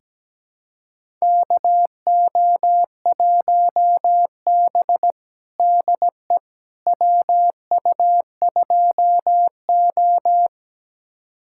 Morse code, heard as a keyed tone, about 17 wpm, 700 Hz. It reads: KO1B DE WU2O
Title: USA Callsign DE Callsign - 4 Characters - 17wpm